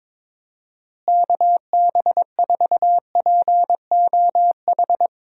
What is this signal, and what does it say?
Morse code, heard as a keyed tone, about 21 wpm, 700 Hz. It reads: KB4POH
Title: USA Call Signs - 6 Characters - Repeated 3x - 22wpm